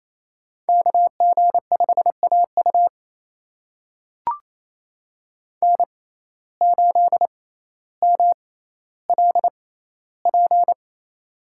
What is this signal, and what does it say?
Morse code, heard as a keyed tone, about 28 wpm, spaced as for 14 wpm, 700 Hz. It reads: KG5AU N8MLP
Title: USA Call Signs - 5 Characters - 6x Character Spacing - 28wpm